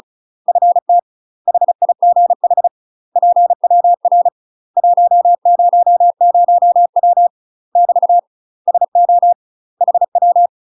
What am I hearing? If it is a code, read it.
Morse code, heard as a keyed tone, about 34 wpm, 700 Hz. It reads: FT HIGH PWR 100W <BT> SO HW
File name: UK MASTER QSOs TRAINING / 35WPM / UK MASTER QSOs 0000